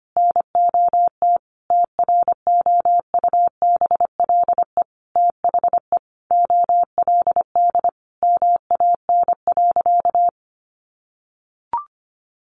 Morse code, heard as a keyed tone, about 25 wpm, 700 Hz. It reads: NOT TROUBLE THE OLD MAN.
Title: Sentences from Top 1200 Words 25wpm